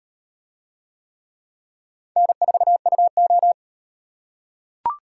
Morse code, heard as a keyed tone, about 37 wpm, 700 Hz. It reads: N4UO